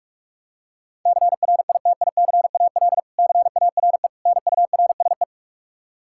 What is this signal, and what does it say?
Morse code, heard as a keyed tone, about 42 wpm, 700 Hz. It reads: CRITICAL CARE NURSE
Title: Practice QSO - Exchange 3 - More Info - 45wpm